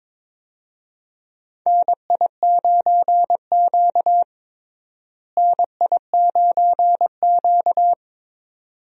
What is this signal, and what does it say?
Morse code, heard as a keyed tone, about 21 wpm, 700 Hz. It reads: NI9Q NI9Q
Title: USA Call Signs - 4 Characters - Repeated 2x - 22wpm